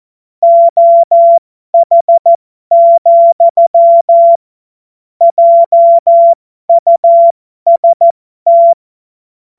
Morse code, heard as a keyed tone, about 14 wpm, spaced as for 12 wpm, 680 Hz. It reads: OH, JUST
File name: Morsekurs / Witze